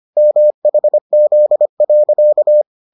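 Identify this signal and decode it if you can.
Morse code, heard as a keyed tone, about 25 wpm, 600 Hz. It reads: MHZ.
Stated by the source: Morse edition of the Foundations of Amateur Radio episodes.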